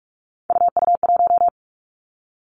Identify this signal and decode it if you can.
Morse code, heard as a keyed tone, about 46 wpm, 700 Hz. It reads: UU1
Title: Three Challenging Letter-Number Combinations 45wpm